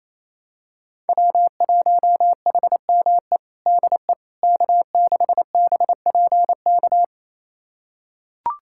Morse code, heard as a keyed tone, about 28 wpm, 700 Hz. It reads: W1HME DE K6BPK
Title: USA Callsign DE Callsign - 5 Characters - 28wpm